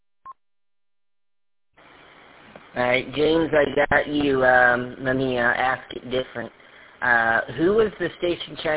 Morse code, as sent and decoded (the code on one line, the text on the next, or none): none